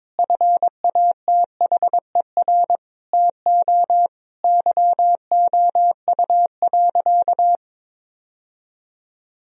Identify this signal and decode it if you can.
Morse code, heard as a keyed tone, about 22 wpm, 700 Hz. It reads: FATHER TO YOU.
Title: Sentences from Top 1000 Words - Review Part 1 - 22wpm